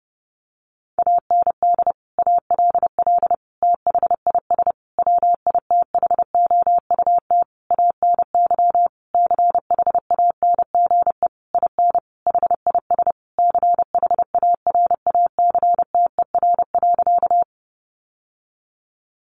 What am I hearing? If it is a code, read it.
Morse code, heard as a keyed tone, about 28 wpm, 700 Hz. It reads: AND ALL THIS WITHOUT ANY CHANGE IN HIS CHARACTER.
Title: Sentences from Top 1200 Words 30wpm